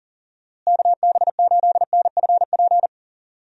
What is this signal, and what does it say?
Morse code, heard as a keyed tone, about 40 wpm, 700 Hz. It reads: KB8NFP
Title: USA Call Signs - 6 Characters - Repeated 2x - 40wpm